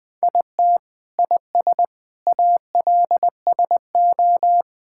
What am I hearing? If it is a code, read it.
Morse code, heard as a keyed tone, about 21 wpm, 700 Hz. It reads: IT IS ALSO
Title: Sets of 8 Words - 20wpm